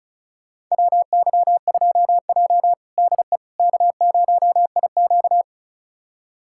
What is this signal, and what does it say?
Morse code, heard as a keyed tone, about 36 wpm, 700 Hz. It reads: WY2J DE K0IQ
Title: USA Callsign DE Callsign - 4 Characters - 35wpm